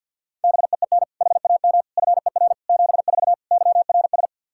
Morse code, heard as a keyed tone, about 49 wpm, 700 Hz. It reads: BEEN HAM FER 74 YRS
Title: Practice QSO - Exchange 3 - More Info - 50wpm